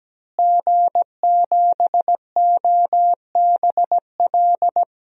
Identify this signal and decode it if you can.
Morse code, heard as a keyed tone, about 17 wpm, 700 Hz. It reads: G7OBL
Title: DXCC-100 Call Signs - 5 Characters - Repeated 3x - 17wpm